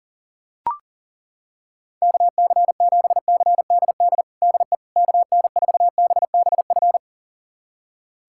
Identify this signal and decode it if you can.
Morse code, heard as a keyed tone, about 40 wpm, 700 Hz. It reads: KC7CDD DE KN4BBF
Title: USA Callsign DE Callsign - 6 Characters - 40wpm